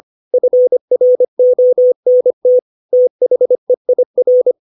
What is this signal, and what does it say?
Morse code, heard as a keyed tone, about 25 wpm, 500 Hz. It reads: FRONT THEIR
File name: 632 words / 632 Words 25WPM